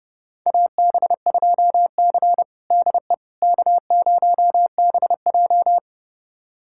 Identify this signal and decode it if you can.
Morse code, heard as a keyed tone, about 32 wpm, 700 Hz. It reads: AB2C DE K0BJ
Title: USA Callsign DE Callsign - 4 Characters - 30wpm